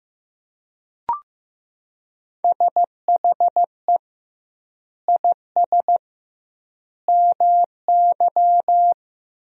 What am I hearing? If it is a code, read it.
Morse code, heard as a keyed tone, about 15 wpm, 700 Hz. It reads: SHE IS MY
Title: 2x Word-Spacing - Sentences from Top 300 Words - 15wpm